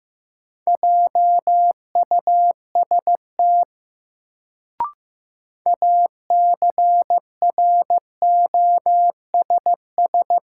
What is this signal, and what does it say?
Morse code, heard as a keyed tone, about 15 wpm, 700 Hz. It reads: JUST ACROSS
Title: Sets of 2 Words - New Word Drill - 15wpm